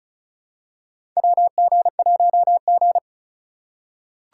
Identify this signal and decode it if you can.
Morse code, heard as a keyed tone, about 36 wpm, 700 Hz. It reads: WG1G
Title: USA Call Signs - 4 Characters - 4x Character Spacing - 35wpm